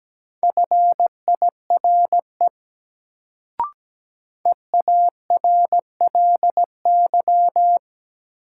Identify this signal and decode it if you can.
Morse code, heard as a keyed tone, about 17 wpm, 700 Hz. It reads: FIRE EARLY
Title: Sentences from Top 400 Words - New Word Drill - 17wpm